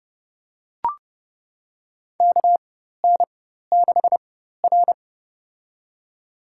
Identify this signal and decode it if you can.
Morse code, heard as a keyed tone, about 30 wpm, spaced as for 14 wpm, 700 Hz. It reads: KN6R
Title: USA Call Signs - 4 Characters - 4x Character Spacing - 30wpm